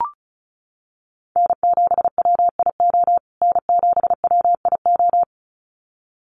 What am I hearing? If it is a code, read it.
Morse code, heard as a keyed tone, about 34 wpm, 700 Hz. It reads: N7WIO N7WIO
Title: USA Call Signs Repeated 2x 35wpm